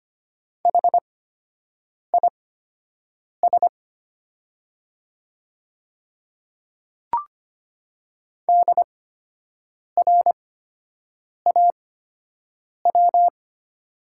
Morse code, heard as a keyed tone, about 25 wpm, spaced as for 7 wpm, 700 Hz. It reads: HIS DRAW
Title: 8x Character Spacing - Top 300 Words - 25wpm